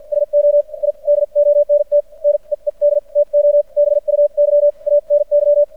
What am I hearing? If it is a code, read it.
Morse code, heard as a keyed tone, about 52 wpm, spaced as for 42 wpm, 580 Hz. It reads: SO I DONT NEED TO BUY ANY
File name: archived / CRSnet